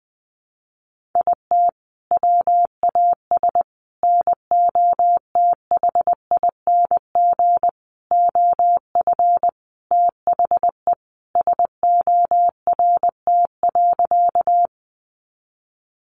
Morse code, heard as a keyed tone, about 19 wpm, 700 Hz. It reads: IT WAS NOTHING OF THE SORT.
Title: Sentences from Top 1300 Words 20wpm